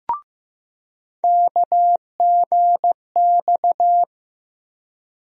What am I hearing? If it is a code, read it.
Morse code, heard as a keyed tone, about 15 wpm, 700 Hz. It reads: KGX
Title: USA Call Sign Postfixes - 3 Letters 15wpm